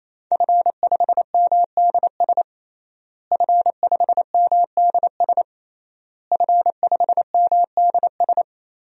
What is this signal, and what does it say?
Morse code, heard as a keyed tone, about 28 wpm, 700 Hz. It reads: F5MDS F5MDS F5MDS
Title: DXCC-100 Call Signs - 5 Characters - Repeated 3x - 28wpm